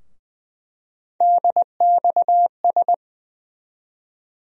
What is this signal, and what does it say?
Morse code, heard as a keyed tone, about 20 wpm, 700 Hz. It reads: DXS
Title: USA Call Sign Postfixes - 3 Letters 20wpm